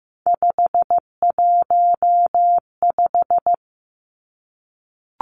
Morse code, heard as a keyed tone, about 15 wpm, 700 Hz. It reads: 515
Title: Three Challenging Letter-Number Combinations 15wpm